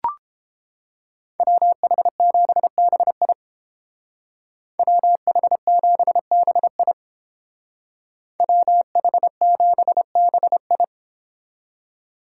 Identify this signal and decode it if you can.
Morse code, heard as a keyed tone, about 29 wpm, 700 Hz. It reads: WH7BI WH7BI WH7BI